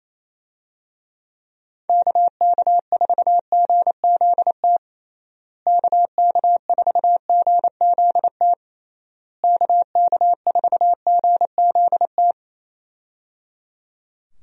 Morse code, heard as a keyed tone, about 28 wpm, 700 Hz. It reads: KK4GZT KK4GZT KK4GZT
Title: USA Call Signs - 6 Characters - Repeated 3x - 28wpm